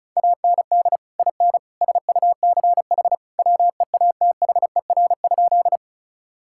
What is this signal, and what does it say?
Morse code, heard as a keyed tone, about 36 wpm, 700 Hz. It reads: AND IN SUCH WEATHER?
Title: Sentences from Top 1000 Words - Review Part 5 - 35wpm